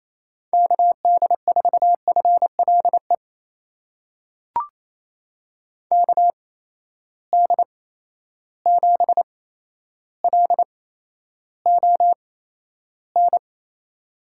Morse code, heard as a keyed tone, about 27 wpm, spaced as for 12 wpm, 700 Hz. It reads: KD4FLE KD7LON
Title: USA Call Signs - 6 Characters - 8x Character Spacing - 28wpm